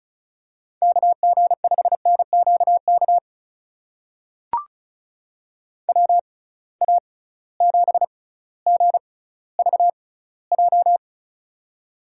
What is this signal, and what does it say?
Morse code, heard as a keyed tone, about 36 wpm, spaced as for 18 wpm, 700 Hz. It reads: KG5NQK WA7GVJ
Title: USA Call Signs - 6 Characters - 6x Character Spacing - 35wpm